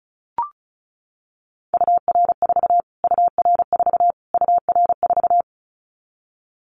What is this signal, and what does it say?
Morse code, heard as a keyed tone, about 34 wpm, 700 Hz. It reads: UR4 UR4 UR4
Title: Callsign Prefixes for Top 25 DXCC - Repeated 3x - 35wpm